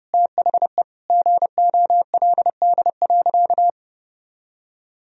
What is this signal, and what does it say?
Morse code, heard as a keyed tone, about 30 wpm, 700 Hz. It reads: THE GOLD.
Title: Sentences from Top 1000 Words - Review Part 3 - 30wpm